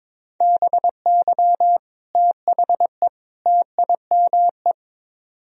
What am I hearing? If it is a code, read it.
Morse code, heard as a keyed tone, about 24 wpm, 700 Hz. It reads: BY THE TIME
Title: Speed-Racing - Sets of 3 Words 22wpm